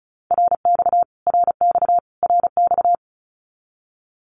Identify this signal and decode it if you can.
Morse code, heard as a keyed tone, about 34 wpm, 700 Hz. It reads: RX RX RX
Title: Abbreviations Repeated 3x - 35wpm